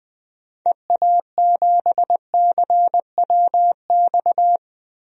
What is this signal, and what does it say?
Morse code, heard as a keyed tone, about 20 wpm, 700 Hz. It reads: EA7CWX
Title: DXCC-100 Call Signs - 6 Characters - Repeated 2x - 20wpm